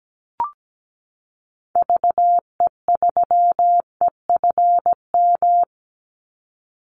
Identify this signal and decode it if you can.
Morse code, heard as a keyed tone, about 17 wpm, 700 Hz. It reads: VE3EFM
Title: Call Signs - 17wpm